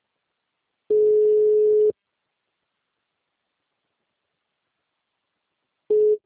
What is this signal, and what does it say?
Ringback tone (Europe, 425 Hz).